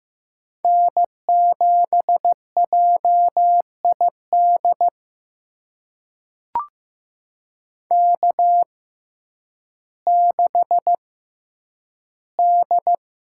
Morse code, heard as a keyed tone, about 15 wpm, spaced as for 9 wpm, 700 Hz. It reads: N7JID K6D
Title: USA Call Signs - 5 Characters - 6x Character Spacing - 15wpm